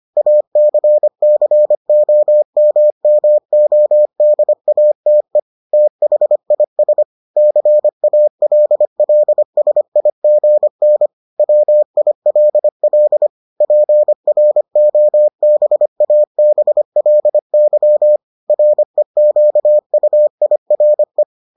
Morse code, heard as a keyed tone, about 25 wpm, 600 Hz. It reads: ACCOMMODATE THIS CALLSIGN WILL PROBABLY REQUIRE